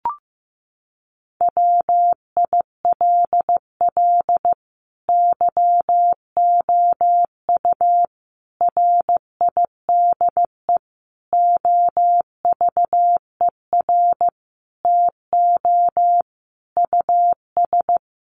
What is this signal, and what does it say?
Morse code, heard as a keyed tone, about 15 wpm, 700 Hz. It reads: WILL YOU RIDE OVER TO US
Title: Sentences from Top 1200 Words 15wpm